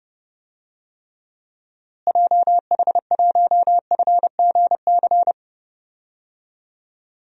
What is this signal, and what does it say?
Morse code, heard as a keyed tone, about 30 wpm, 700 Hz. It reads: JH1FGC
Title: DXCC-100 Call Signs - 6 Characters - Repeated 3x - 30wpm